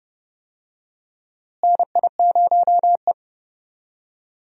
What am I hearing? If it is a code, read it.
Morse code, heard as a keyed tone, about 30 wpm, 700 Hz. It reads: NI0E
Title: USA Call Signs - 4 Characters - Repeated 2x - 30wpm